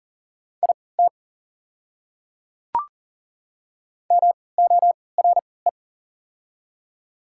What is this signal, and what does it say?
Morse code, heard as a keyed tone, about 39 wpm, spaced as for 22 wpm, 690 Hz. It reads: IT MORE
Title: 3x Character Spacing - Top 100 Words - 40wpm